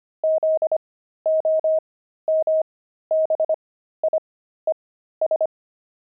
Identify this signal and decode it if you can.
Morse code, heard as a keyed tone, about 25 wpm, spaced as for 13 wpm, 630 Hz. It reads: ZOMBIES
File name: pangram-fs-13-cs-25-jaded